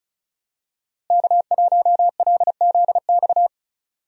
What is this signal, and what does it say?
Morse code, heard as a keyed tone, about 34 wpm, 700 Hz. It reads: K1LZX
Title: USA Call Signs - 5 Characters - Repeated 2x - 35wpm